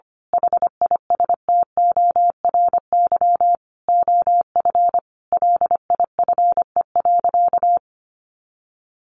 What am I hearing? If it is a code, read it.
Morse code, heard as a keyed tone, about 25 wpm, 700 Hz. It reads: HISTORY OF LIFE.
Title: A Sound Of Thunder 25wpm